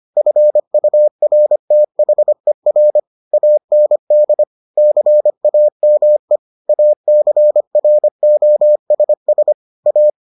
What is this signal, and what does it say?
Morse code, heard as a keyed tone, about 25 wpm, 600 Hz. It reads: FURTHER AND CAME ACROSS A